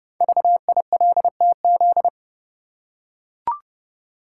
Morse code, heard as a keyed tone, about 30 wpm, 690 Hz. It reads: VILTZ
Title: Code Groups - 5 Letters - 30wpm